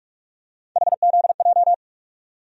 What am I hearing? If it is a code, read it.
Morse code, heard as a keyed tone, about 43 wpm, 700 Hz. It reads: HZJ
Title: USA Call Sign Postfixes - 3 Letters 45wpm